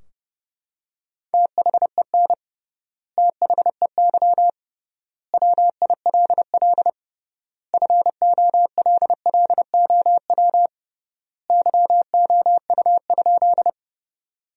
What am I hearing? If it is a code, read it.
Morse code, heard as a keyed tone, about 30 wpm, 700 Hz. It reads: THEN THEY WILL FOLLOW YOU?
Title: 3x Word-Spacing - Sentences from Top 200 Words - 30wpm